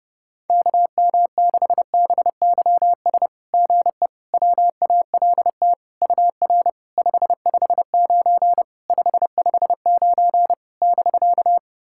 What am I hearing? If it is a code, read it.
Morse code, heard as a keyed tone, about 30 wpm, 690 Hz. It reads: KM6BYS GE WALT UR 559 559 <BK>
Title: SOTA QSO Practice 30wpm